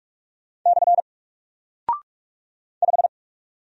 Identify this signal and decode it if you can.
Morse code, heard as a keyed tone, about 44 wpm, 700 Hz. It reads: / 5
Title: Single Character (Letter:Number:Punctuation:Prosign) 45wpm